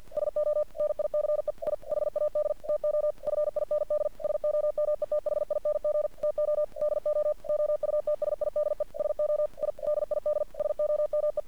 Morse code, heard as a keyed tone, about 50 wpm, 600 Hz. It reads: SO NICE I HAD TO FIND SOMETHING TO DO OUTSIDE SO I DID SOME
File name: archived / CRSnet